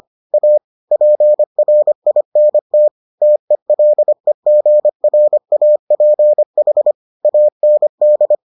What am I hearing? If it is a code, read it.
Morse code, heard as a keyed tone, about 25 wpm, 600 Hz. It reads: A PRINT TELEGRAPH AND